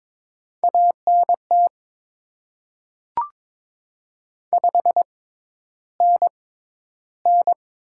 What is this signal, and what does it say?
Morse code, heard as a keyed tone, about 22 wpm, spaced as for 11 wpm, 690 Hz. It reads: ANT 5NN
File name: Top 20 QSO Elements - 6x Character Spacing - 22wpm